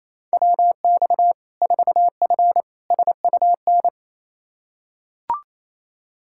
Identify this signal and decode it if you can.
Morse code, heard as a keyed tone, about 27 wpm, 700 Hz. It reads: WX 4F SUN
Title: Practice QSO - Exchange 2 - Info - 28wpm